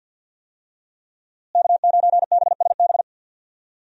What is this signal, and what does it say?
Morse code, heard as a keyed tone, about 46 wpm, 700 Hz. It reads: K9BSB
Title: USA Call Signs - 5 Characters - Repeated 3x - 50wpm